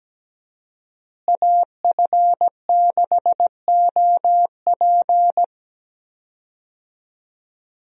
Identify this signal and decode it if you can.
Morse code, heard as a keyed tone, about 17 wpm, 700 Hz. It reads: AF6OP
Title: USA Call Signs - 5 Characters - Repeated 2x - 17wpm